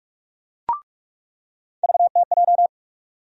Morse code, heard as a keyed tone, about 41 wpm, 700 Hz. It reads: VTJ
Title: USA Call Sign Postfixes - 3 Letters 45wpm